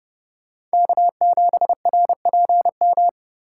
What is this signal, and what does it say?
Morse code, heard as a keyed tone, about 30 wpm, 700 Hz. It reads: K7RPM
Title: USA Call Signs - 5 Characters - 2x Character Spacing - 30wpm